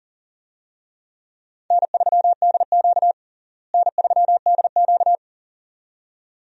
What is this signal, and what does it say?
Morse code, heard as a keyed tone, about 37 wpm, 700 Hz. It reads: N3DQ N3DQ
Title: USA Call Signs - 4 Characters - Repeated 2x - 40wpm